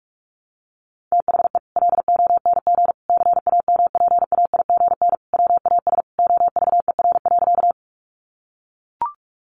Morse code, heard as a keyed tone, about 44 wpm, 700 Hz. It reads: THE LONG CAMPAIGN WAS OVER.
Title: Sentences from Top 1200 Words 45wpm